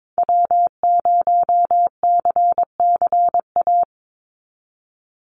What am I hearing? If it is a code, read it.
Morse code, heard as a keyed tone, about 22 wpm, 700 Hz. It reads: W0CCA
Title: NA SOTA Activator Callsigns 22wpm